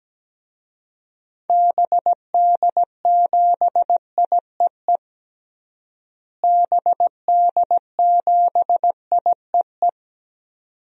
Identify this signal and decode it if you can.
Morse code, heard as a keyed tone, about 17 wpm, 700 Hz. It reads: BD7IEE BD7IEE
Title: DXCC-100 Call Signs - 6 Characters - Repeated 2x - 17wpm